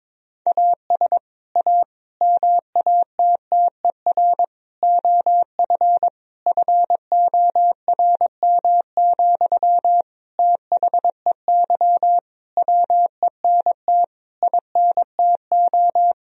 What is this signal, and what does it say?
Morse code, heard as a keyed tone, about 22 wpm, 700 Hz. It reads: AS A MATTER OF FORM, THEY WENT INTO
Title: Sentences from Top 1000 Words - Review Part 1 - 22wpm